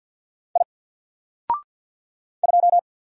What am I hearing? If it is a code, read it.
Morse code, heard as a keyed tone, about 50 wpm, 700 Hz. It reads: I 2
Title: Single Letter-Number 50wpm